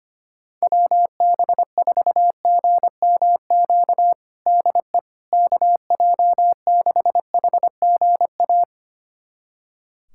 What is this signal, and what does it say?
Morse code, heard as a keyed tone, about 25 wpm, 700 Hz. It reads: WB4GMQ DE KJ6HGA
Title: USA Callsign DE Callsign - 6 Characters - 25wpm